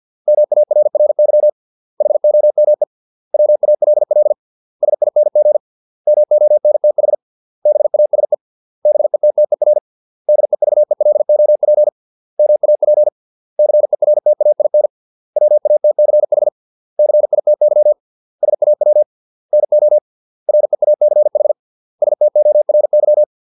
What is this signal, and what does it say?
Morse code, heard as a keyed tone, about 50 wpm, 600 Hz. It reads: MARRY HOME WALL SING MONTH BASE BETTER DEVELOP MAP CERTAIN WATCH CITY SAW NO REACH STORY